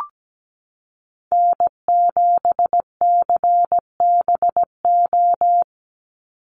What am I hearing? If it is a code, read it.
Morse code, heard as a keyed tone, about 17 wpm, 700 Hz. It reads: N7CBO
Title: USA Call Signs 17wpm